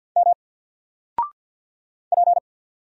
Morse code, heard as a keyed tone, about 48 wpm, 700 Hz. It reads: M P
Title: Single Character (Letter:Number:Punctuation:Prosign) 50wpm